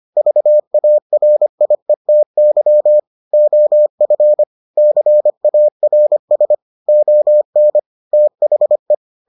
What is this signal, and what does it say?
Morse code, heard as a keyed tone, about 25 wpm, 600 Hz. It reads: VARIETY OF CARS ON THE